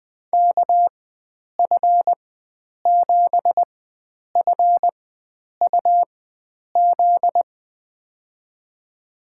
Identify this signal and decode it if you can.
Morse code, heard as a keyed tone, about 20 wpm, spaced as for 9 wpm, 700 Hz. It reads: KF7FUZ
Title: USA Call Signs - 6 Characters - 4x Character Spacing - 20wpm